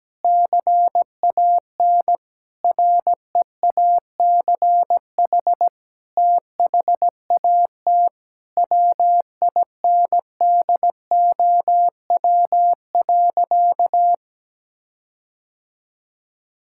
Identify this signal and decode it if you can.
Morse code, heard as a keyed tone, about 17 wpm, 700 Hz. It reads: CAN REACH THAT WINDOW.
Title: Sentences from Top 1600 Words 17wpm